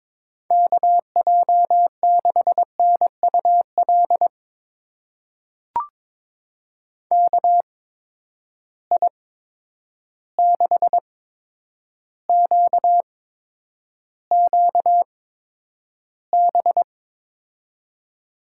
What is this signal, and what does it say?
Morse code, heard as a keyed tone, about 22 wpm, spaced as for 9 wpm, 700 Hz. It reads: KJ6NUL KI6QQB
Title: USA Call Signs - 6 Characters - 8x Character Spacing - 22wpm